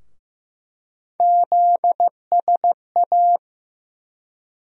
Morse code, heard as a keyed tone, about 15 wpm, 700 Hz. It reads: ZSA
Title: USA Call Sign Postfixes - 3 Letters 15wpm